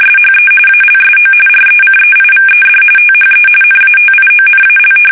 RTTY, about 45.45 baud. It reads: SHE ALSO RECEIVED THE NATIONAL
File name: fsk850 / battleshipnj